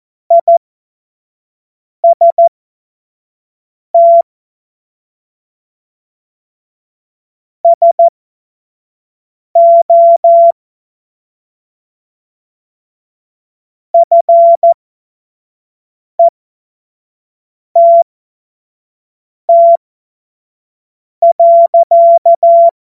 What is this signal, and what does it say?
Morse code, heard as a keyed tone, about 14 wpm, spaced as for 5 wpm, 680 Hz. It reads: IST SO FETT.
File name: Morsekurs / Witze / Witz 18. Mai 2021, 25 F70